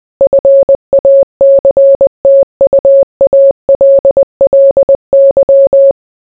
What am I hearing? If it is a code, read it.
Morse code, heard as a keyed tone, about 20 wpm, 550 Hz. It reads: FACTUALLY